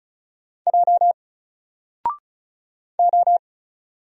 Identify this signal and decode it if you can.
Morse code, heard as a keyed tone, about 35 wpm, 700 Hz. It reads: J O
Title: Single Character (Letter:Number:Punctuation:Prosign) 35wpm